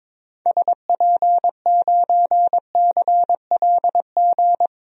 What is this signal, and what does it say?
Morse code, heard as a keyed tone, about 22 wpm, 700 Hz. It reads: SP9CLG
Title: DXCC-100 Call Signs - 6 Characters - 22wpm